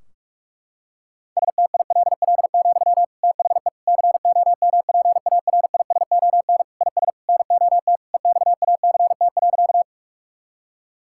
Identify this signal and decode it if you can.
Morse code, heard as a keyed tone, about 46 wpm, 700 Hz. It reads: STILL, THE COMPARISON IS NOT EXACT.
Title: Sentences from Top 1600 Words 45wpm